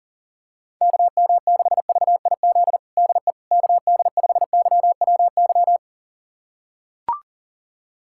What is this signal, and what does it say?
Morse code, heard as a keyed tone, about 40 wpm, 700 Hz. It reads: KM6VIZ DE KD5YWY
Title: USA Callsign DE Callsign - 6 Characters - 40wpm